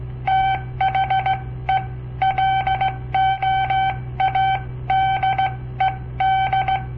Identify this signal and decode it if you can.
Morse code, heard as a keyed tone, about 16 wpm, spaced as for 14 wpm, 770 Hz. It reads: THE LOADED
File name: papertapes / local / Code Practice-01